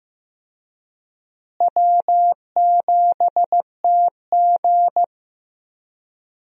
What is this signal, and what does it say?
Morse code, heard as a keyed tone, about 15 wpm, 700 Hz. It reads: W7TG
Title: USA Call Signs - 4 Characters - Repeated 2x - 15wpm